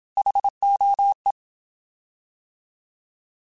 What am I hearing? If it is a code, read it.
Morse code, heard as a keyed tone, about 27 wpm, 800 Hz. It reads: HOE
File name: Alphabetical / Alphabetical mp3s / Hs @25